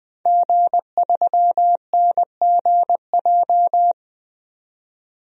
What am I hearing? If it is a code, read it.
Morse code, heard as a keyed tone, about 21 wpm, 700 Hz. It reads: G3NGJ
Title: DXCC-100 Call Signs - 5 Characters - 20wpm